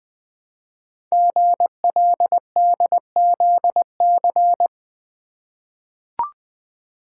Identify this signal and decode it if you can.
Morse code, heard as a keyed tone, about 20 wpm, 700 Hz. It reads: GLDZC
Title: Code Groups - 5 Letters - 20wpm